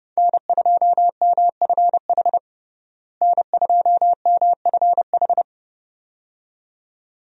Morse code, heard as a keyed tone, about 30 wpm, 700 Hz. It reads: N2MFH N2MFH
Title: USA Call Signs - 5 Characters - Repeated 2x - 30wpm